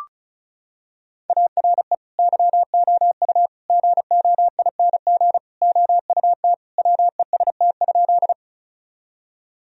Morse code, heard as a keyed tone, about 36 wpm, 700 Hz. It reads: ARE YOU GOING OUT WEST?
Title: Sentences from Top 1000 Words - Review Part 2 - 35wpm